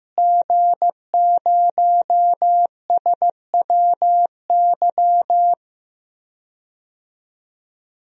Morse code, heard as a keyed tone, about 16 wpm, 700 Hz. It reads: G0SWY
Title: DXCC-100 Call Signs - 5 Characters - Repeated 2x - 15wpm